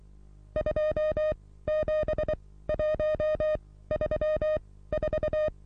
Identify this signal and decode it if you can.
Morse code, heard as a keyed tone, about 23 wpm, spaced as for 16 wpm, 620 Hz. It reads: 27134
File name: repos / Morsekurs / DJ4UF Morsen27